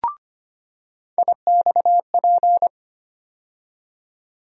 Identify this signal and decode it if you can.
Morse code, heard as a keyed tone, about 25 wpm, 700 Hz. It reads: IXP
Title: USA Call Sign Postfixes - 3 Letters 25wpm